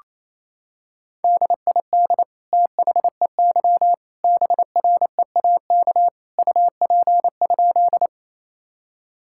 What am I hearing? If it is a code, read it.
Morse code, heard as a keyed tone, about 27 wpm, 700 Hz. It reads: DID THEY BREAK UP?
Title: Sentences from Top 1000 Words - Review Part 6 - 28wpm